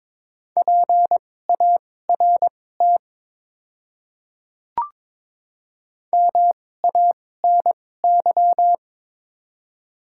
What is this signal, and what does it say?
Morse code, heard as a keyed tone, about 22 wpm, spaced as for 16 wpm, 700 Hz. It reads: PART MANY
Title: 2x Character Spacing - Top 200 Words - 22wpm